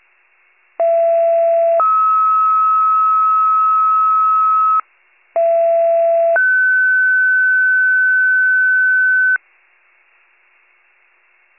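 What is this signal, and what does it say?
Two-tone page, heard as a one-second tone, then a three-second tone, about 670 then 1290 Hz, about 670 then 1500 Hz. It is heard from 0.8 to 9.4 s.
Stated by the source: County-wide paging alert, WCFM pager archive.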